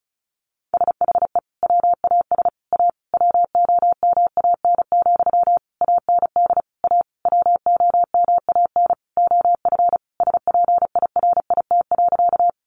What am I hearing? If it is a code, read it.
Morse code, heard as a keyed tone, about 34 wpm, 700 Hz. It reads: SHE WAS A WOMAN, AND A WOMAN OF SPIRIT.
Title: Sentences from Top 1400 Words 35wpm